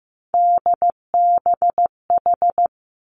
Morse code, heard as a keyed tone, about 15 wpm, 700 Hz. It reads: DBH
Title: Three Challenging Letter-Number Combinations 15wpm